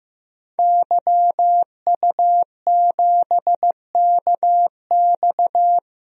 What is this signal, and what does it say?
Morse code, heard as a keyed tone, about 15 wpm, 700 Hz. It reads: YU7KX
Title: DXCC-100 Call Signs - 5 Characters - 15wpm